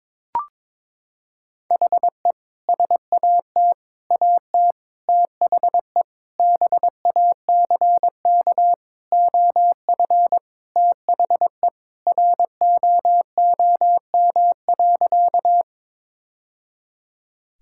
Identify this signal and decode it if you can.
Morse code, heard as a keyed tone, about 22 wpm, 700 Hz. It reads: HE SAT AT THE BACK OF THE ROOM.
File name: practice / 22wpm / Sentences with 3-Word Phrases - 22wpm